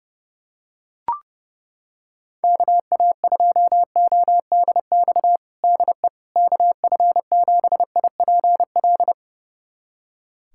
Morse code, heard as a keyed tone, about 30 wpm, 700 Hz. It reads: KA2ODX DE KF7IPL
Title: USA Callsign DE Callsign - 6 Characters - 30wpm